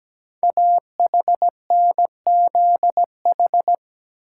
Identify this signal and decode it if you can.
Morse code, heard as a keyed tone, about 17 wpm, 700 Hz. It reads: AHNZH
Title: Code Groups - 5 Letters - 17wpm